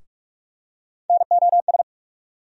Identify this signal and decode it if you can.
Morse code, heard as a keyed tone, about 44 wpm, 700 Hz. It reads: NOS
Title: USA Call Sign Postfixes - 3 Letters 45wpm